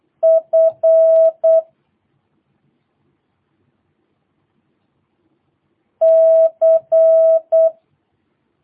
Morse code, heard as a keyed tone, about 8 wpm, 650 Hz. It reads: F C
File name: jh8cbh / 2022cw